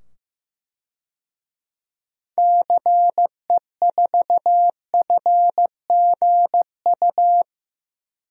Morse code, heard as a keyed tone, about 15 wpm, 700 Hz. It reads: CE4FGU
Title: DXCC-100 Call Signs - 6 Characters - 15wpm